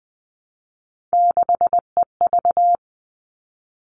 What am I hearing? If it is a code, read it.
Morse code, heard as a keyed tone, about 20 wpm, 700 Hz. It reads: 6EV
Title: All Cut Numbers - 3-Digit 20wpm